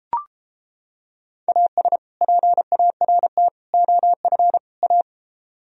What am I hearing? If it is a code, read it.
Morse code, heard as a keyed tone, about 29 wpm, 700 Hz. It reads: AS PART OF A